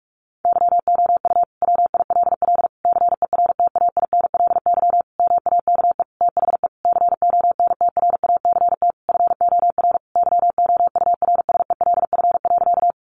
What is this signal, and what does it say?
Morse code, heard as a keyed tone, about 44 wpm, 700 Hz. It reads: YOU WILL CERTAINLY MAKE THE CONTRACT FOR YOURSELF.
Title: Sentences from Top 1400 Words 45wpm